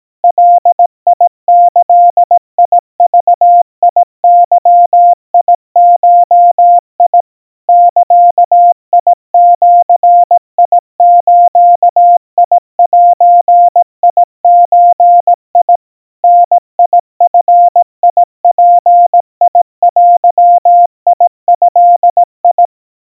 Morse code (Japanese Wabun code), heard as a keyed tone, about 18 wpm, 700 Hz. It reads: ガギグゲゴ ザジズゼゾ ダヂヅデド
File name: 60 cpm 700 Hz Crispy